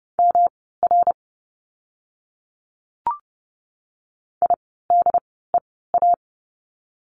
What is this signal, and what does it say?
Morse code, heard as a keyed tone, about 30 wpm, spaced as for 17 wpm, 700 Hz. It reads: MR IDEA
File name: course / 30:10wpm / 108 Lesson 18f - Letter m - Words 30wpm